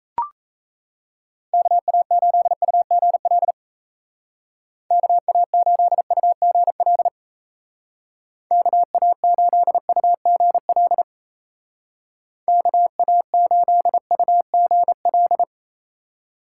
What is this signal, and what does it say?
Morse code, heard as a keyed tone, about 37 wpm, 700 Hz. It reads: KA8UGL KA8UGL KA8UGL KA8UGL